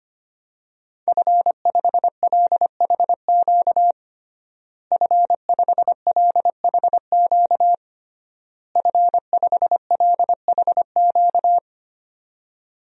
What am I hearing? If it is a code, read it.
Morse code, heard as a keyed tone, about 25 wpm, 700 Hz. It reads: F5LHQ F5LHQ F5LHQ
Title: DXCC-100 Call Signs - 5 Characters - Repeated 3x - 25wpm